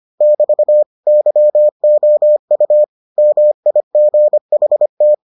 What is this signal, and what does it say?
Morse code, heard as a keyed tone, about 25 wpm, 600 Hz. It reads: <BT> YOU MIGHT